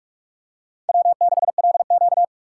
Morse code, heard as a keyed tone, about 44 wpm, 700 Hz. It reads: W6LQ
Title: USA Call Signs - 4 Characters - 3x Character Spacing - 45wpm